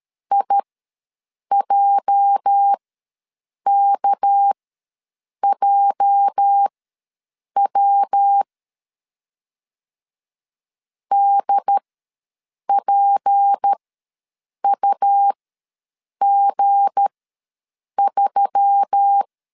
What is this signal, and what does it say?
Morse code, heard as a keyed tone, about 13 wpm, spaced as for 7 wpm, 800 Hz. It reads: IJKJW DPUG3
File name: LEZIONI / 8 YJ45 / 8 YJ ws 12wpm 3342